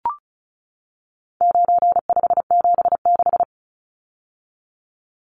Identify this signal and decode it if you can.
Morse code, heard as a keyed tone, about 34 wpm, 700 Hz. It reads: 9576
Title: Numbers - 4-Digit 35wpm